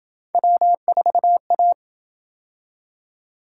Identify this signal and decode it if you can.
Morse code, heard as a keyed tone, about 26 wpm, 700 Hz. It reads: W4A